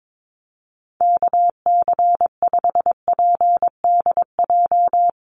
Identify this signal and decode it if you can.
Morse code, heard as a keyed tone, about 22 wpm, 700 Hz. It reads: KC5PDJ
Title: USA Call Signs 22wpm